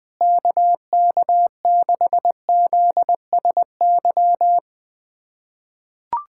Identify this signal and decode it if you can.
Morse code, heard as a keyed tone, about 20 wpm, 700 Hz. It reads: KK6ZSY
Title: USA Callsign DE Callsign 20wpm